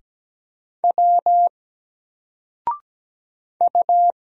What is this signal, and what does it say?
Morse code, heard as a keyed tone, about 17 wpm, 700 Hz. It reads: W U
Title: Single Character (Letter:Number:Punctuation:Prosign) 17wpm